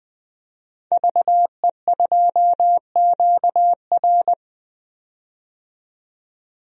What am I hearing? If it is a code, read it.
Morse code, heard as a keyed tone, about 21 wpm, 700 Hz. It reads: VE2QR
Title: DXCC-100 Call Signs - 5 Characters - 20wpm